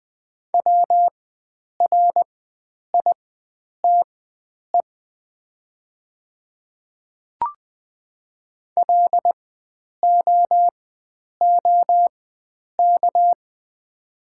Morse code, heard as a keyed tone, about 20 wpm, spaced as for 9 wpm, 700 Hz. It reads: WRITE LOOK
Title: 4x Character Spacing - Top 100 Words - 20wpm